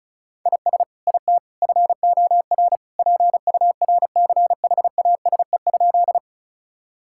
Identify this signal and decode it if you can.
Morse code, heard as a keyed tone, about 36 wpm, 700 Hz. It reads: IS IT FOR PURCHASE?
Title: Sentences from Top 1000 Words - Review Part 5 - 35wpm